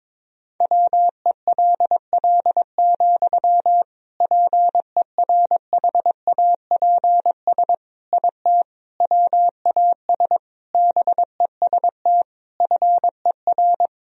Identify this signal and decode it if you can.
Morse code, heard as a keyed tone, about 22 wpm, 700 Hz. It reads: WELL, PERHAPS IT WAS BEST FER
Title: Sentences from Top 500 Words and Abbreviations 22wpm